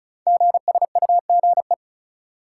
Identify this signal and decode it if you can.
Morse code, heard as a keyed tone, about 36 wpm, 690 Hz. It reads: GSUGE
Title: Code Groups - 5 Letters - 35wpm